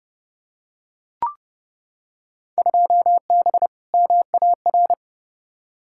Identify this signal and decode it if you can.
Morse code, heard as a keyed tone, about 30 wpm, 700 Hz. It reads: 2B MAR
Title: Field Day Exchanges Encore - 30wpm